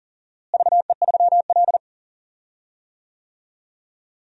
Morse code, heard as a keyed tone, about 40 wpm, 700 Hz. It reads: VE3L
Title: DXCC-100 Call Signs - 4 Characters - Repeated 2x - 40wpm